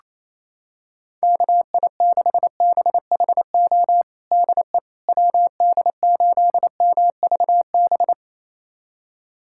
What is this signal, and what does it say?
Morse code, heard as a keyed tone, about 28 wpm, 700 Hz. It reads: KI6BHO DE WD8MVB
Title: USA Callsign DE Callsign 28wpm